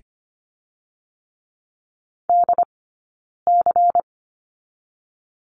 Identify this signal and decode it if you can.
Morse code, heard as a keyed tone, about 25 wpm, 700 Hz. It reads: D C
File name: course / 25:10wpm / 092 Lesson 16d - Letter c - Two Characters 25wpm